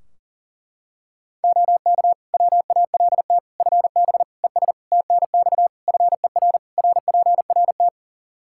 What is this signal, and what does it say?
Morse code, heard as a keyed tone, about 40 wpm, 700 Hz. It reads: OK WALT FB ES TNX FER RPRT
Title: Practice QSO - Exchange 2 - Info - 40wpm